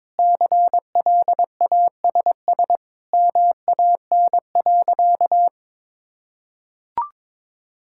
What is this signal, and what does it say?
Morse code, heard as a keyed tone, about 22 wpm, 700 Hz. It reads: CLASS MAN.
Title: Sentences from Top 1000 Words - Review Part 6 - 22wpm